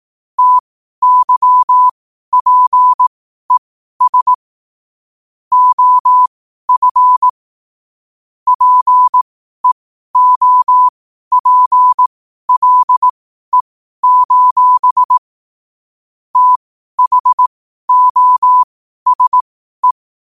Morse code, heard as a keyed tone, about 18 wpm, spaced as for 12 wpm, 1000 Hz. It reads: TYPES OF PEOPLE: THOSE
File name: qotdfiles / 12wpm